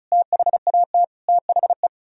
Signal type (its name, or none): Morse code